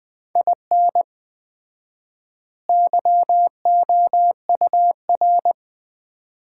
Morse code, heard as a keyed tone, about 20 wpm, 690 Hz. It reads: IN YOUR
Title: 4x Word-Spacing - Sets of 2 Words - 20wpm